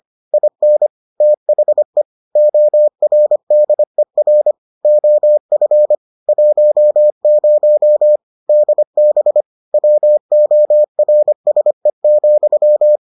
Morse code, heard as a keyed tone, about 25 wpm, 600 Hz. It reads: IN THE ORDER OF 10 DB WORSE,